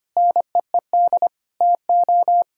Morse code, heard as a keyed tone, about 25 wpm, 700 Hz. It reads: NEED TO
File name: practice / 25wpm / Sentences with 3-Word Phrases - 25wpm